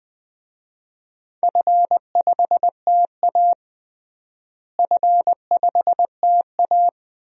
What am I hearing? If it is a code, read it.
Morse code, heard as a keyed tone, about 20 wpm, 700 Hz. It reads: F5TA F5TA
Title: DXCC-100 Call Signs - 4 Characters - Repeated 2x - 20wpm